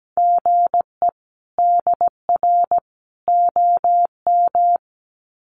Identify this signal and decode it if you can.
Morse code, heard as a keyed tone, about 17 wpm, 700 Hz. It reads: GE DR OM
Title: Common QSO Phrases 17wpm